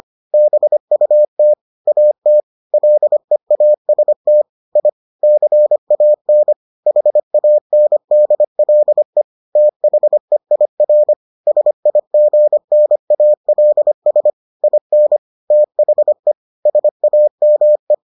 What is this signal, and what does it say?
Morse code, heard as a keyed tone, about 25 wpm, 600 Hz. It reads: BUT AT LEAST I CAN HANDLE THEIR SIGNALS IN THE SAME